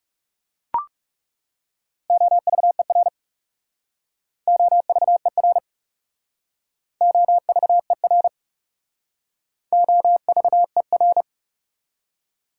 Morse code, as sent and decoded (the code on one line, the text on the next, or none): --- ...- . .-. / --- ...- . .-. / --- ...- . .-. / --- ...- . .-.
OVER OVER OVER OVER